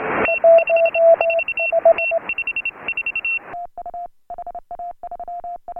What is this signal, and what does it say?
Morse code, heard as a keyed tone, about 30 wpm, spaced as for 24 wpm, 2700 Hz, with noise. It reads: T54 T54
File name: AUDIO2018 / HA3FMR CW